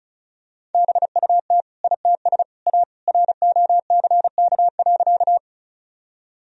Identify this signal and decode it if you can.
Morse code, heard as a keyed tone, about 34 wpm, 700 Hz. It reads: BUT ITS A ROCK.
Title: Sentences from Top 1000 Words - Review Part 2 - 35wpm